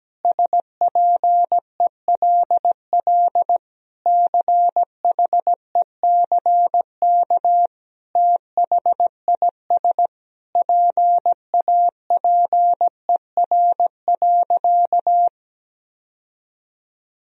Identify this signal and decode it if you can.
Morse code, heard as a keyed tone, about 17 wpm, 700 Hz. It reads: SPELL CHECK THIS PAPER.
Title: Sentences from Top 1000 Words - Review Part 3 - 17wpm